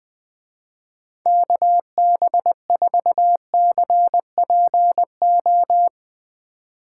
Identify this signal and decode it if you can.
Morse code, heard as a keyed tone, about 19 wpm, 700 Hz. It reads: KB4CPO
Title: Speed Racing - USA 2x3 Call Signs 17wpm